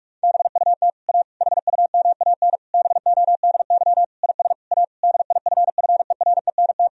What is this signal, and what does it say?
Morse code, heard as a keyed tone, about 46 wpm, 700 Hz. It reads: BUT A HUMAN BODY IS A DIFFERENT
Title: Sentences from Top 1000 Words - Review Part 3 - 45wpm